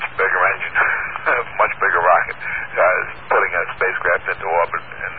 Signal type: Emergency Alert System (SAME) tones